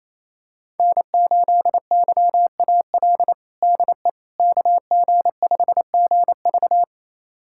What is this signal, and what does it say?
Morse code, heard as a keyed tone, about 28 wpm, 700 Hz. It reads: N8YAL DE KG5GV
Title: USA Callsign DE Callsign - 5 Characters - 28wpm